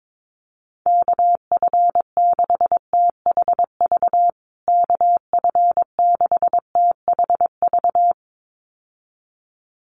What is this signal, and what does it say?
Morse code, heard as a keyed tone, about 22 wpm, 700 Hz. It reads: KF6THV KF6THV
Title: USA Call Signs Repeated 2x 22wpm